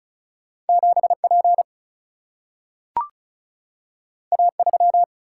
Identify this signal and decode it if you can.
Morse code, heard as a keyed tone, about 36 wpm, 700 Hz. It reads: 7 P A 3
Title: Single-Letter Single-Number 35wpm